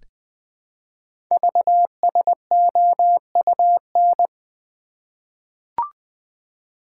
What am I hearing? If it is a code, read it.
Morse code, heard as a keyed tone, about 20 wpm, 700 Hz. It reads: VSOUN
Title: Code Groups - 5 Letters - 20wpm